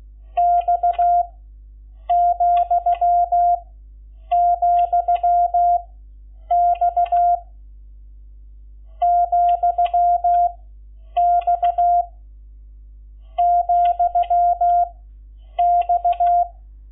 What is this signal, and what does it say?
Morse code, heard as a keyed tone, about 16 wpm, 700 Hz. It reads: X , , X , X , X
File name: Lecciones / CLASE Nº11